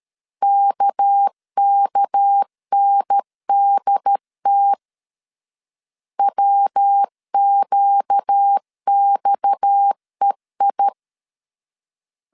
Morse code, heard as a keyed tone, about 13 wpm, 800 Hz. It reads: KKNDT WQXEI
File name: CW / DET / 1 6 1500C S B 12wpm 3431